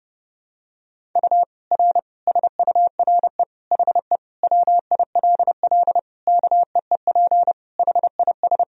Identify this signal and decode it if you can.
Morse code, heard as a keyed tone, about 30 wpm, 700 Hz. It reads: U R SURE HE WILL KEEP HIS
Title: Sentences from Top 300 Words and Abbreviations 30wpm